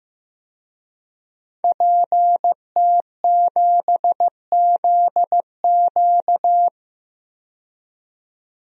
Morse code, heard as a keyed tone, about 15 wpm, 700 Hz. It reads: PT7ZQ
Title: DXCC-100 Call Signs - 5 Characters - 15wpm